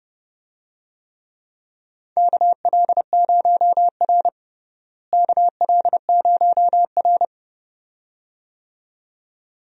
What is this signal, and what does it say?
Morse code, heard as a keyed tone, about 30 wpm, 700 Hz. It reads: KL0R KL0R
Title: USA Call Signs - 4 Characters - Repeated 2x - 30wpm